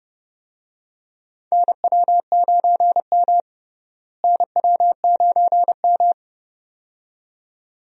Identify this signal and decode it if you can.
Morse code, heard as a keyed tone, about 30 wpm, 700 Hz. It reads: NW9M NW9M
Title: USA Call Signs - 4 Characters - Repeated 2x - 30wpm